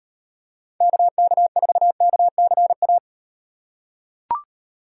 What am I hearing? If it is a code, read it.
Morse code, heard as a keyed tone, about 38 wpm, 700 Hz. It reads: KK4KCA